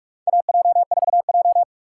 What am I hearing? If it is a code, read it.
Morse code, heard as a keyed tone, about 46 wpm, 700 Hz. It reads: AJ4J
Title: USA Call Signs - 4 Characters - 3x Character Spacing - 45wpm